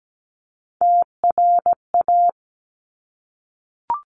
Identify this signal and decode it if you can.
Morse code, heard as a keyed tone, about 17 wpm, 700 Hz. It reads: TRA
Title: Top Three Letter Combinations in Words 17wpm